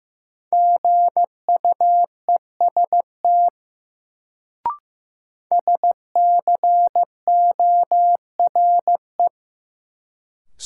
Morse code, heard as a keyed tone, about 15 wpm, 700 Hz. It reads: GUEST SCORE
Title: Sentences from Top 1600 Words - New Word Drill - 15wpm